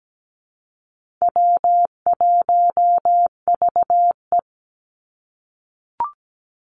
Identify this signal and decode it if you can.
Morse code, heard as a keyed tone, about 17 wpm, 700 Hz. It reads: W1VE
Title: Call Signs - 17wpm